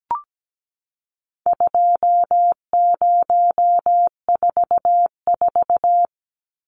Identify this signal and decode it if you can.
Morse code, heard as a keyed tone, about 17 wpm, 700 Hz. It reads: 2044
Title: Numbers - 4-Digit 17wpm